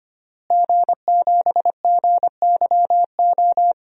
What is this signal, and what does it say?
Morse code, heard as a keyed tone, about 25 wpm, 700 Hz. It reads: G7GYO
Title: DXCC-100 Call Signs - 5 Characters - Repeated 2x - 25wpm